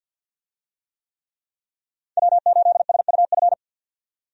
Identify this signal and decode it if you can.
Morse code, heard as a keyed tone, about 51 wpm, 700 Hz. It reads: W8SUF
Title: USA Call Signs - 5 Characters - 50wpm